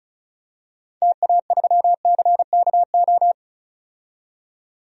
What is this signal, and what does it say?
Morse code, heard as a keyed tone, about 35 wpm, 700 Hz. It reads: TA3CKO
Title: DXCC-100 Call Signs - 6 Characters - 35wpm